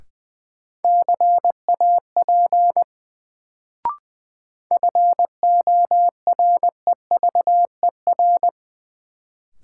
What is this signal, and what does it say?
Morse code, heard as a keyed tone, about 20 wpm, 700 Hz. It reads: CAP FOREVER
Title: Sentences from Top 1600 Words - New Word Drill Alt - 20wpm